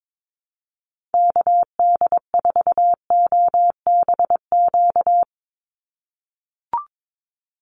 Morse code, heard as a keyed tone, about 22 wpm, 700 Hz. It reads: KD4OBQ
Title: USA Call Signs 22wpm